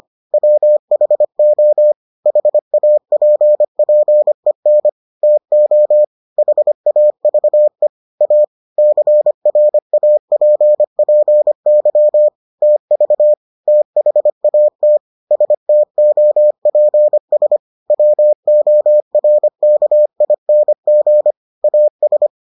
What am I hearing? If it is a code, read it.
Morse code, heard as a keyed tone, about 25 wpm, 600 Hz. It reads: WHO HAPPEN TO HAVE A CRAPPY TV THAT STOPS WORKING AS